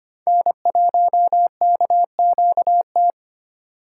Morse code, heard as a keyed tone, about 25 wpm, 700 Hz. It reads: N1KQT
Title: USA Call Signs - 5 Characters - 8x Character Spacing - 25wpm